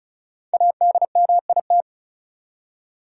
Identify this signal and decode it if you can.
Morse code, heard as a keyed tone, about 34 wpm, 700 Hz. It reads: ADMIT
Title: Five Letter Words - 35wpm